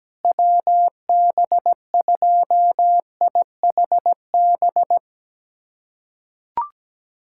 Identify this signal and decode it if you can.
Morse code, heard as a keyed tone, about 17 wpm, 700 Hz. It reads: WB2IHB
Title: USA Callsign DE Callsign 17wpm